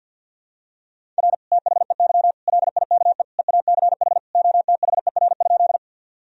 Morse code, heard as a keyed tone, about 51 wpm, 700 Hz. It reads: R THEY LIKE EACH OTHER?
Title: Sentences from Top 200 Words and Abbreviations 50wpm